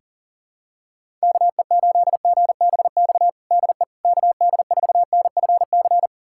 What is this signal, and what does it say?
Morse code, heard as a keyed tone, about 40 wpm, 700 Hz. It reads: KE8GBX DE KD4NFC
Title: USA Callsign DE Callsign - 6 Characters - 40wpm